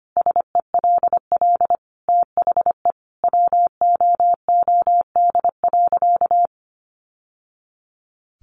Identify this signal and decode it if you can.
Morse code, heard as a keyed tone, about 25 wpm, 700 Hz. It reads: SELL THE WOOD.
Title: Sentences from Top 1100 Words 25wpm